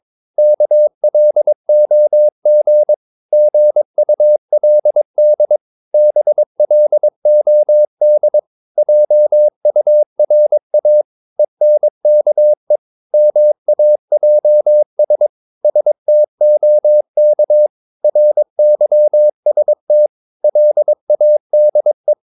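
Morse code, heard as a keyed tone, about 22 wpm, 600 Hz. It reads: KLOG GULD BLOD JURA ENKE MAJS STOK RYST LADE
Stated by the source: Morse practice of short Danish words at 22 wpm, from CW med Gnister.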